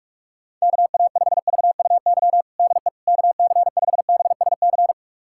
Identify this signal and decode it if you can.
Morse code, heard as a keyed tone, about 46 wpm, 700 Hz. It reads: KA5VUY DE KC5BSC
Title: USA Callsign DE Callsign - 6 Characters - 45wpm